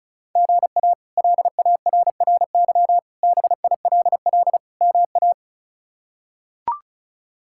Morse code, heard as a keyed tone, about 36 wpm, 700 Hz. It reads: GA LARRY BILL MA
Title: K1USN SST QSO Practice - 35wpm